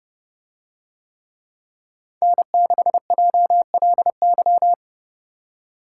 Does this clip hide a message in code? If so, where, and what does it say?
Yes: Morse code from 2.2 to 4.7 s, heard as a keyed tone, about 28 wpm, 700 Hz. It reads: N6JLY